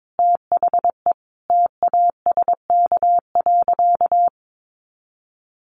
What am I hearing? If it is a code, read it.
Morse code, heard as a keyed tone, about 22 wpm, 700 Hz. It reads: THE TASK.
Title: Sentences from Top 1400 Words 22wpm